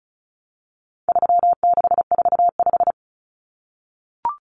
Morse code, heard as a keyed tone, about 34 wpm, 700 Hz. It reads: 3645
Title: Common Cut Numbers - 4-Digit 35wpm